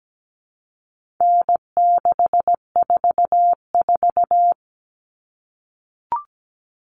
Morse code, heard as a keyed tone, about 17 wpm, 700 Hz. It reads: N644
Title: Common Cut Numbers - 4-Digit 17wpm